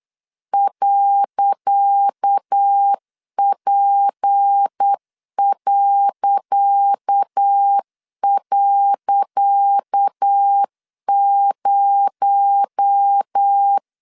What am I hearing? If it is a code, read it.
Morse code, heard as a keyed tone, about 8 wpm, 800 Hz. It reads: .P..0
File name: LEZIONI / 10 SIGNS / 10C DOT 8wpm 3649